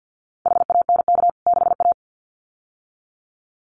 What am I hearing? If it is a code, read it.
Morse code, heard as a keyed tone, about 50 wpm, 700 Hz. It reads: HANK 6A
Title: CWT Exchange 50wpm